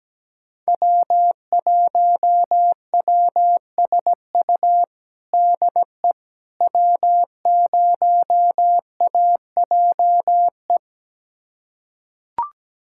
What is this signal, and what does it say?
Morse code, heard as a keyed tone, about 17 wpm, 700 Hz. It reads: W1WSU DE W0AJE
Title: USA Callsign DE Callsign - 5 Characters - 17wpm